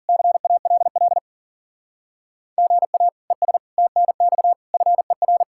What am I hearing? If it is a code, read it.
Morse code, heard as a keyed tone, about 36 wpm, 700 Hz. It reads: CALL GA ES TNX FER
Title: Speed-Racing Common QSO Phrases - 40wpm